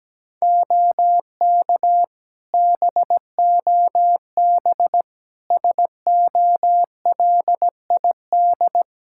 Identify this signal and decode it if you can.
Morse code, heard as a keyed tone, about 17 wpm, 700 Hz. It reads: OK BOB SOLID
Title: Practice QSO - Exchange 3 - More Info - 17wpm